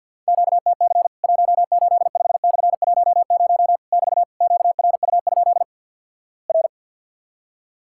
Morse code, heard as a keyed tone, about 50 wpm, 700 Hz. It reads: QTC 185/10 <BT> QRU? R
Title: Worked All Europe DX Contest - QTC Exchange - 50wpm